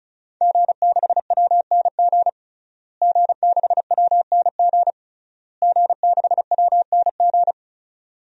Morse code, heard as a keyed tone, about 34 wpm, 700 Hz. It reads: G6WNG G6WNG G6WNG
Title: DXCC-100 Call Signs - 5 Characters - Repeated 3x - 35wpm